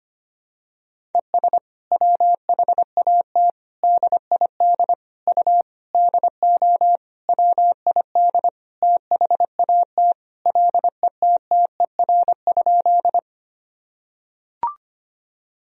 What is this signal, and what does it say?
Morse code, heard as a keyed tone, about 25 wpm, 700 Hz. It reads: ES WHAT DID U DO WID THAT LETTER?
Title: Sentences from Top 300 Words and Abbreviations 25wpm